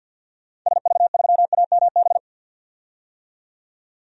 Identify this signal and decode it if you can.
Morse code, heard as a keyed tone, about 46 wpm, 700 Hz. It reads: SV3AMB
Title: DXCC-100 Call Signs - 6 Characters - 50wpm